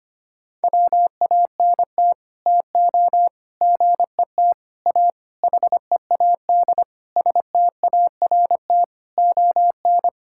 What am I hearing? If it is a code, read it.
Morse code, heard as a keyed tone, about 25 wpm, 700 Hz. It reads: WANT TO GET A HEAD START ON
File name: Sets of 9 Words - 25wpm